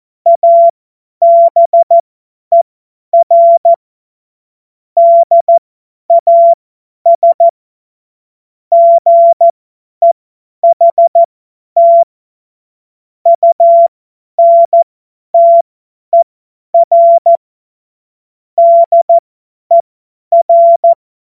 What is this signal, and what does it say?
Morse code, heard as a keyed tone, about 14 wpm, spaced as for 10 wpm, 680 Hz. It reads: ABER DAS GEHT UNTER DER
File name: Morsekurs / Witze / Witz 26 März 2024 50 F70